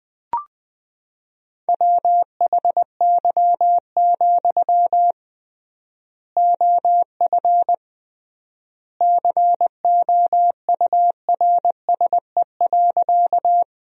Morse code, heard as a keyed tone, about 21 wpm, 700 Hz. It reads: WHY, OF COURSE.
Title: 3x Word-Spacing - Sentences from Top 500 Words - 20wpm